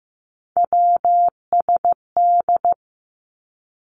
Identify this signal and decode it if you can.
Morse code, heard as a keyed tone, about 15 wpm, 700 Hz. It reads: WSD
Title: Three Challenging Letter-Number Combinations 15wpm